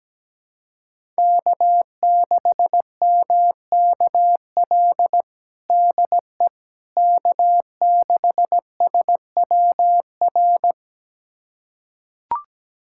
Morse code, heard as a keyed tone, about 17 wpm, 700 Hz. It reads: K6MKL DE K6SWR
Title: USA Callsign DE Callsign - 5 Characters - 17wpm